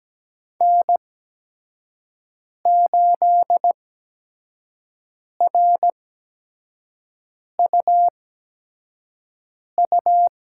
Morse code, heard as a keyed tone, about 17 wpm, spaced as for 5 wpm, 700 Hz. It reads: N8RUU
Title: USA Call Signs - 8x Character Spacing - 17wpm